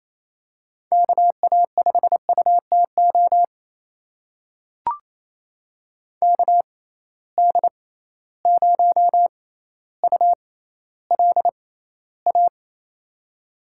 Morse code, heard as a keyed tone, about 28 wpm, spaced as for 14 wpm, 700 Hz. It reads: KA5UTO KD0ULA
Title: USA Call Signs - 6 Characters - 6x Character Spacing - 28wpm